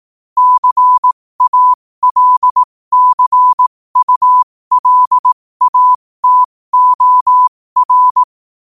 Morse code, heard as a keyed tone, about 18 wpm, spaced as for 15 wpm, 1000 Hz. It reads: CALCULATOR
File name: qotdfiles / 15wpm